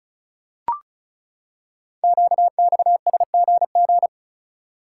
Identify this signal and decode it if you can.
Morse code, heard as a keyed tone, about 36 wpm, 700 Hz. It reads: QXSGG
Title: Code Groups - 5 Letter:Numbers - 35wpm